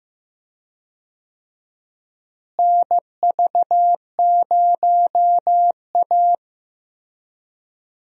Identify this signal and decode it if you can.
Morse code, heard as a keyed tone, about 15 wpm, 700 Hz. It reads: NV0A
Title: USA Call Signs - 4 Characters - Repeated 2x - 15wpm